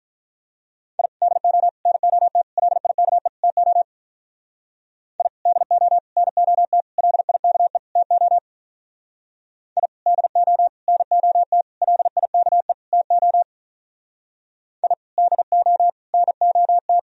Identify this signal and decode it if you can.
Morse code, heard as a keyed tone, about 43 wpm, 700 Hz. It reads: I DO NOT LIKE TO I DO NOT LIKE TO I DO NOT LIKE TO I DO NOT
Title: Speed-Racing - Sets of 5 Words 35wpm